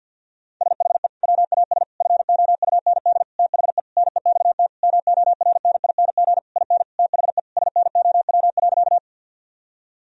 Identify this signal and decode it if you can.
Morse code, heard as a keyed tone, about 50 wpm, 700 Hz. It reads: SHE WAS FOUND THE NEXT MORNING IN THE SNOW.
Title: Sentences from Top 1000 Words - Review Part 3 - 50wpm